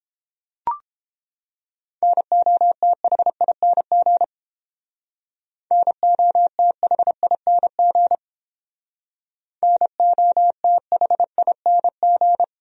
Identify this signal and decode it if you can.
Morse code, heard as a keyed tone, about 30 wpm, 700 Hz. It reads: NOTHING NOTHING NOTHING